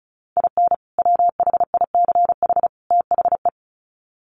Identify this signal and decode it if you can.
Morse code, heard as a keyed tone, about 34 wpm, 700 Hz. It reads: IN WHICH THE
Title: Sets of 3 Words 35wpm